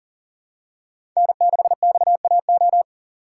Morse code, heard as a keyed tone, about 40 wpm, 700 Hz. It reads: N6XAO
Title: USA Call Signs - 3x Character Spacing - 40wpm